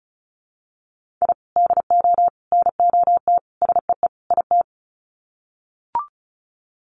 Morse code, heard as a keyed tone, about 34 wpm, 700 Hz. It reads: I DO NOT SEE IT
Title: Sets of 5 Words Encore2 35wpm